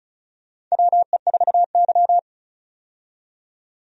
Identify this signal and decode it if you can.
Morse code, heard as a keyed tone, about 36 wpm, 700 Hz. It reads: WE4Y
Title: USA Call Signs - 4 Characters - 35wpm